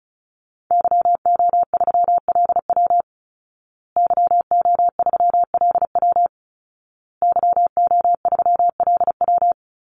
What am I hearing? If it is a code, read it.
Morse code, heard as a keyed tone, about 34 wpm, 700 Hz. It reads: YO3LW YO3LW YO3LW
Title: Call Signs Repeated 2x - 35wpm